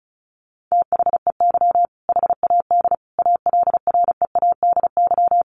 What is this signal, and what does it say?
Morse code, heard as a keyed tone, about 34 wpm, 700 Hz. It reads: THEY HAD ALREADY
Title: Sentences from Top 1200 Words 35wpm